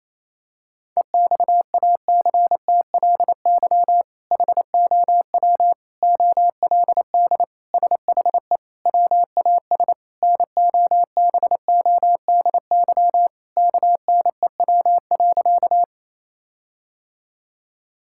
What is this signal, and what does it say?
Morse code, heard as a keyed tone, about 28 wpm, 700 Hz. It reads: EXACTLY HOW OLD SHE WAS NOBODY KNEW.
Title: Sentences from Top 1600 Words 28wpm